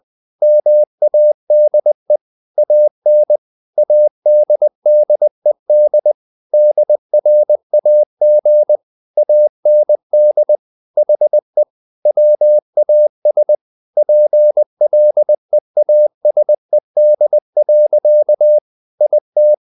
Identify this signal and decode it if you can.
Morse code, heard as a keyed tone, about 20 wpm, 600 Hz. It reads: MADE AN ADDED DRAG AND HE WAS PLEASED. IT
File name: the old man and the sea / the old man and the sea part 1 20wpm